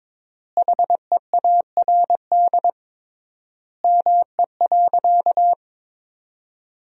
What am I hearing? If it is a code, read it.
Morse code, heard as a keyed tone, about 22 wpm, 700 Hz. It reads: HEARD ME.
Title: 3x Word-Spacing - Sentences from Top 400 Words - 22wpm